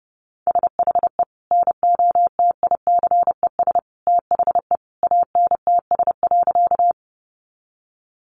Morse code, heard as a keyed tone, about 30 wpm, 700 Hz. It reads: SHE NOTICES THE ANTS.
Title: Easy Phrases 30wpm